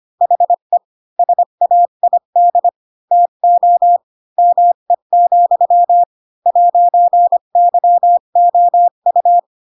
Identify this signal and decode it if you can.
Morse code, heard as a keyed tone, about 25 wpm, 700 Hz. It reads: HE SAID TO ME, 'YOU
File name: Pinocchio / Pinocchio ch7 20 25 ik1hge